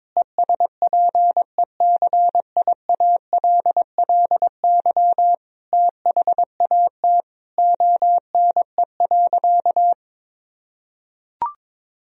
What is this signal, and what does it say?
Morse code, heard as a keyed tone, about 22 wpm, 700 Hz. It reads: ESPECIALLY THAT ONE.
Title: Sentences from Top 1000 Words - Review Part 4 - 22wpm